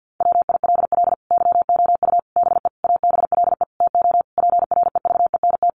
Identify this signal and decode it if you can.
Morse code, heard as a keyed tone, about 50 wpm, 700 Hz. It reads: WILL YOU BE ABLE TO PREVENT
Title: Sentences from Top 1100 Words 50wpm